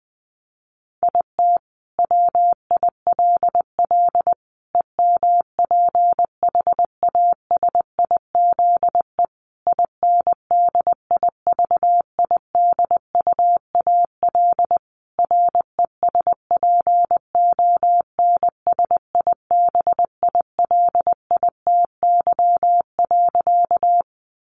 Morse code, heard as a keyed tone, about 19 wpm, 700 Hz. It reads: IT WILL EMPHASIZE INDIVIDUAL RESPONSIBILITY.
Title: Sentences from Top 1500 Words 20wpm